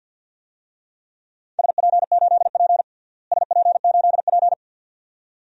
Morse code, heard as a keyed tone, about 48 wpm, 700 Hz. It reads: SP8P SP8P
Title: DXCC-100 Call Signs - 4 Characters - Repeated 2x - 50wpm